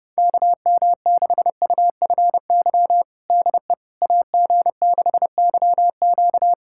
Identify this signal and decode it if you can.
Morse code, heard as a keyed tone, about 30 wpm, 700 Hz. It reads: KM6UFY DE AG6YQ
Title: USA Callsign DE Callsign 30wpm